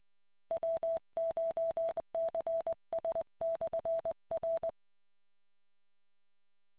Morse code, heard as a keyed tone, about 21 wpm, 660 Hz. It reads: WCHCS/R